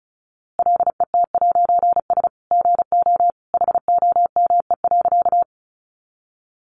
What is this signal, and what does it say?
Morse code, heard as a keyed tone, about 34 wpm, 700 Hz. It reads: LET'S GO HOME.
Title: A Sound Of Thunder 35wpm